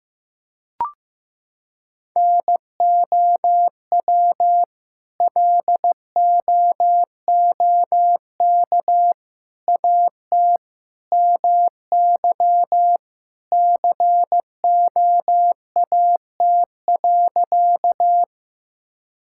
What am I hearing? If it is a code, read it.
Morse code, heard as a keyed tone, about 15 wpm, 700 Hz. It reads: NOW LOOK AT MY COAT.
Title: Sentences from Top 1600 Words 15wpm